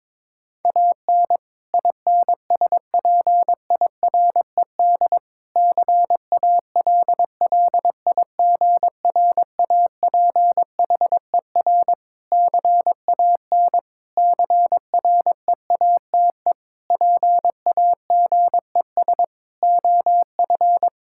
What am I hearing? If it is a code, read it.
Morse code, heard as a keyed tone, about 22 wpm, 700 Hz. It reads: AN INSPIRED CALLIGRAPHER CAN CREATE PAGES OF
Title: Pangram Challenge v2 - 22wpm